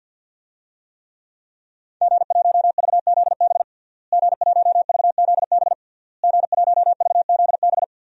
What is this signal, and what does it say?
Morse code, heard as a keyed tone, about 53 wpm, 700 Hz. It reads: G1VZB G1VZB G1VZB
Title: DXCC-100 Call Signs - 5 Characters - Repeated 3x - 50wpm